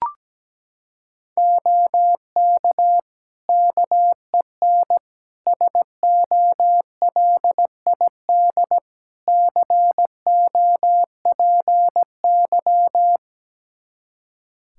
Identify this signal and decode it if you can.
Morse code, heard as a keyed tone, about 17 wpm, 700 Hz. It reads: OK KEN SOLID COPY
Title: Practice QSO - Exchange 3 - More Info - 17wpm